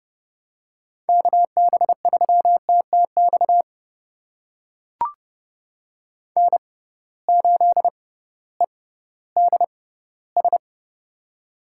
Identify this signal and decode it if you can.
Morse code, heard as a keyed tone, about 30 wpm, spaced as for 16 wpm, 700 Hz. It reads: KB3TTX N8EDS
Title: USA Call Signs - 6x Character Spacing - 30wpm